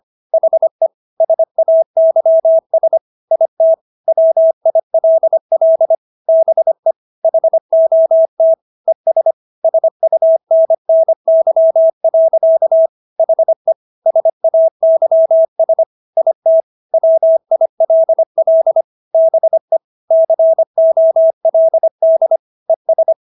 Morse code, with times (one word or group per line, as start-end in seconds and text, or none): 0.3-0.9 HE
1.2-3.0 SAYS
3.3-3.7 IT
4.1-6.0 WILL
6.3-6.9 BE
7.2-8.5 HOT
8.9-9.3 ES
9.6-12.9 SUNNY.
13.2-13.7 HE
14.1-15.8 SAYS
16.2-16.6 IT
16.9-18.8 WILL
19.1-19.8 BE
20.1-22.4 COLD
22.7-23.1 ES